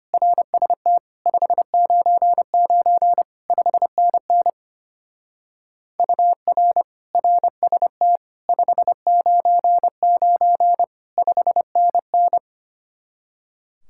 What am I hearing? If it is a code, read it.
Morse code, heard as a keyed tone, about 31 wpm, 700 Hz. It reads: RST 599 5NN UR RST 599 5NN